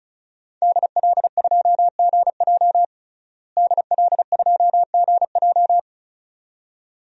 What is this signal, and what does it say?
Morse code, heard as a keyed tone, about 36 wpm, 700 Hz. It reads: DL2GJ DL2GJ
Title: DXCC-100 Call Signs - 5 Characters - Repeated 2x - 35wpm